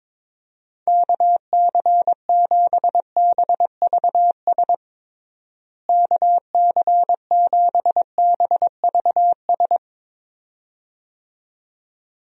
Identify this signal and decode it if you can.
Morse code, heard as a keyed tone, about 22 wpm, 700 Hz. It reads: KC7BVS KC7BVS
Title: USA Call Signs - 6 Characters - Repeated 2x - 22wpm